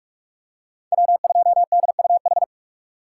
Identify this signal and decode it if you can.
Morse code, heard as a keyed tone, about 46 wpm, 700 Hz. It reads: W2DUH
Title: USA Call Signs - 2x Character Spacing - 45wpm